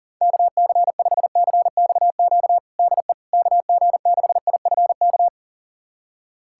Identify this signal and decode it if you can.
Morse code, heard as a keyed tone, about 37 wpm, 700 Hz. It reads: KC5CXQ DE KG6IFK
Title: USA Callsign DE Callsign 40wpm